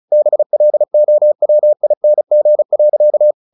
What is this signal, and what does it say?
Morse code, heard as a keyed tone, about 34 wpm, 600 Hz. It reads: BLOWING.
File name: the old man and the sea / the old man and the sea 1 35wpm